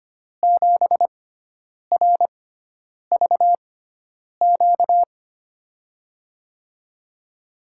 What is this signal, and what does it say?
Morse code, heard as a keyed tone, about 25 wpm, spaced as for 9 wpm, 700 Hz. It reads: 7RVQ
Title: USA Call Signs - 6 Characters - 6x Character Spacing - 25wpm